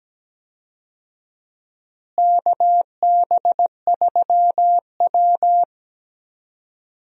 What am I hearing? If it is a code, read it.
Morse code, heard as a keyed tone, about 17 wpm, 700 Hz. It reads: KB3W
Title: USA Call Signs - 4 Characters - Repeated 2x - 17wpm